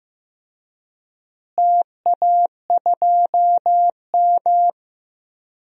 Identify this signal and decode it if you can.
Morse code, heard as a keyed tone, about 15 wpm, 700 Hz. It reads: TA2M
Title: DXCC-100 Call Signs - 4 Characters - Repeated 3x - 15wpm